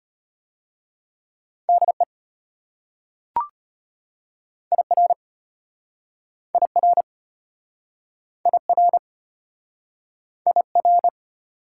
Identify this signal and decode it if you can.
Morse code, heard as a keyed tone, about 34 wpm, 690 Hz. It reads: DE IR IR IR IR